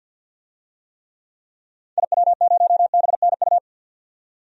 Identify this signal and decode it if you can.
Morse code, heard as a keyed tone, about 51 wpm, 700 Hz. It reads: IW0BNU
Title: DXCC-100 Call Signs - 6 Characters - 50wpm